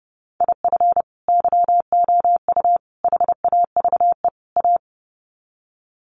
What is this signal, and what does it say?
Morse code, heard as a keyed tone, about 30 wpm, 700 Hz. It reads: IF YOU HAVE A
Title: Sets of 4 Words 30wpm